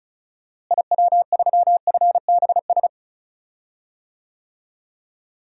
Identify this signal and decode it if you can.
Morse code, heard as a keyed tone, about 36 wpm, 700 Hz. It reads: IW3FBS
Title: DXCC-100 Call Signs - 6 Characters - Repeated 2x - 35wpm